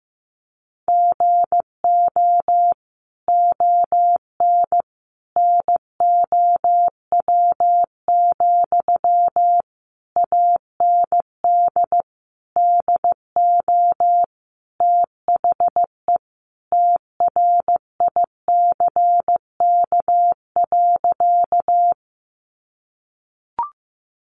Morse code, heard as a keyed tone, about 15 wpm, 700 Hz. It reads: GO ON NOW, AND DO THE TRICK.
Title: Sentences from Top 1500 Words 15wpm